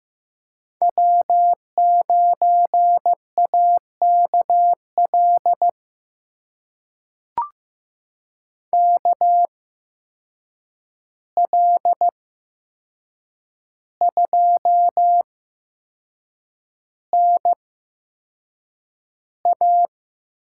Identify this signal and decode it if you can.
Morse code, heard as a keyed tone, about 15 wpm, spaced as for 6 wpm, 700 Hz. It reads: W9AKL KL2NA
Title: USA Call Signs - 5 Characters - 8x Character Spacing - 15wpm